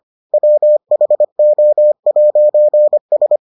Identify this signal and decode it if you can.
Morse code, heard as a keyed tone, about 25 wpm, 600 Hz. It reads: WHO'S